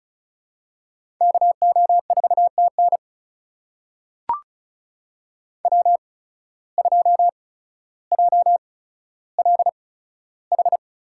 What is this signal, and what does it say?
Morse code, heard as a keyed tone, about 36 wpm, spaced as for 15 wpm, 700 Hz. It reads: KO4TN W2JLH
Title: USA Call Signs - 8x Character Spacing - 35wpm